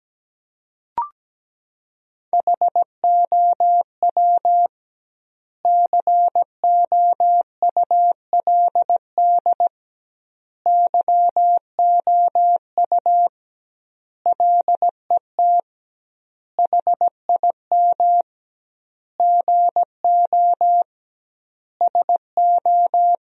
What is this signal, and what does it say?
Morse code, heard as a keyed tone, about 17 wpm, 700 Hz. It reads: HOW COULD YOU LET HIM GO SO
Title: 2x Word-Spacing - Sentences from Top 300 Words - 17wpm